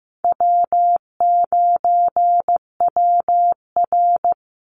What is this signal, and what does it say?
Morse code, heard as a keyed tone, about 15 wpm, 700 Hz. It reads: W9WR
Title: Easy Call Signs - 15wpm